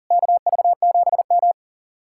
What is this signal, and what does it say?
Morse code, heard as a keyed tone, about 50 wpm, 700 Hz. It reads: KV7M
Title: Speed-Racing Common QSO Phrases Encore - 40wpm